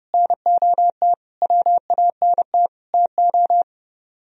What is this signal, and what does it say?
Morse code, heard as a keyed tone, about 35 wpm, 700 Hz. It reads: NOT WANT TO
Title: Speed-Racing - Sets of 5 Words 30wpm